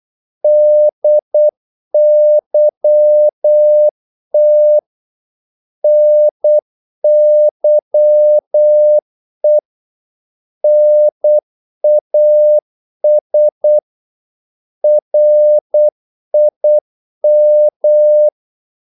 Morse code, heard as a keyed tone, about 8 wpm, 600 Hz. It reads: DYT NYE NAS RIM
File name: Korte ord DK 8wpm